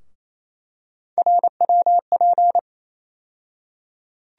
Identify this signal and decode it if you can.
Morse code, heard as a keyed tone, about 28 wpm, 700 Hz. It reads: RWP
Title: USA Call Sign Postfixes - 3 Letters 28wpm